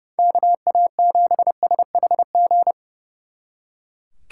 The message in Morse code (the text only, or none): KA7SHG